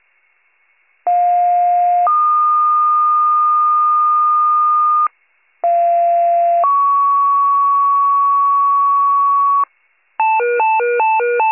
Two-tone page, heard as a one-second tone, then a three-second tone, about 710 then 1220 Hz, about 690 then 1120 Hz. It is heard from 1.1 to 9.6 s.